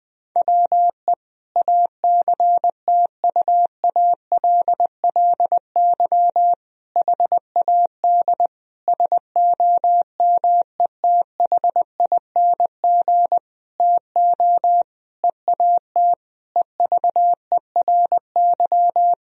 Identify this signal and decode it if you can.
Morse code, heard as a keyed tone, about 20 wpm, 700 Hz. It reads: WE ACTUALLY HAD SOMETHING TO EAT EVERY
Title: Sentences from Top 1000 Words - Review Part 4 - 20wpm